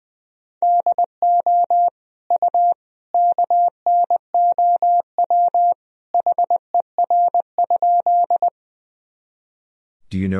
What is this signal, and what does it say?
Morse code, heard as a keyed tone, about 20 wpm, 700 Hz. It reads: DO U KNOW HER?
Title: Sentences from Top 100 Words and Abbreviations 20wpm